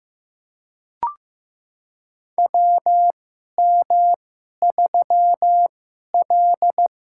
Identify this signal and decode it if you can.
Morse code, heard as a keyed tone, about 15 wpm, spaced as for 11 wpm, 700 Hz. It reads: WM3L
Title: USA Call Signs - 4 Characters - 2x Character Spacing - 15wpm